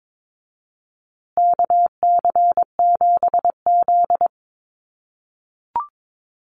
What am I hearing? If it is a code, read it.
Morse code, heard as a keyed tone, about 22 wpm, 700 Hz. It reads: KC7Z
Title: USA Call Signs Repeated 3x 22wpm